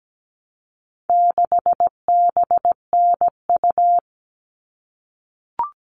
Morse code, heard as a keyed tone, about 17 wpm, 700 Hz. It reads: 6BNU
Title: All Cut Numbers - 4-Digit 17wpm